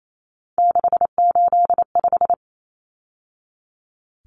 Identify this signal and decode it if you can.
Morse code, heard as a keyed tone, about 28 wpm, 700 Hz. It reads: 685
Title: Numbers - 3-Digit 28wpm